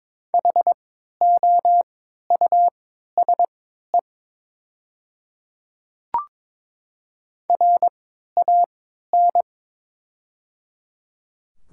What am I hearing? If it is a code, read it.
Morse code, heard as a keyed tone, about 23 wpm, spaced as for 13 wpm, 700 Hz. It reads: HOUSE RAN
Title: 3x Character Spacing - Top 500 Words - 22wpm